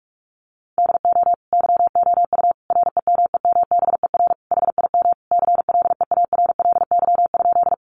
Morse code, heard as a keyed tone, about 44 wpm, 700 Hz. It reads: DO YOU REMEMBER HIM CLEARLY?
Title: Sentences from Top 1300 Words 45wpm